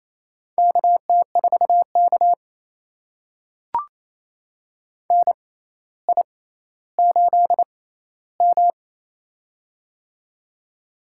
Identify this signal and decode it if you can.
Morse code, heard as a keyed tone, about 28 wpm, spaced as for 14 wpm, 700 Hz. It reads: KT4K NI8M
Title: USA Call Signs - 4 Characters - 6x Character Spacing - 28wpm